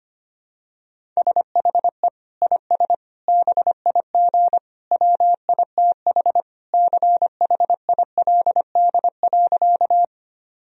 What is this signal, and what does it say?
Morse code, heard as a keyed tone, about 25 wpm, 700 Hz. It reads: SHE IS BIG WITH CHILD.
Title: Sentences from Top 1000 Words - Review Part 1 - 25wpm